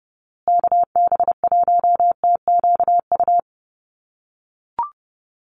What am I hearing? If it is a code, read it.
Morse code, heard as a keyed tone, about 30 wpm, 700 Hz. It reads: KB1TQU
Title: USA Call Signs Repeated 2x 30wpm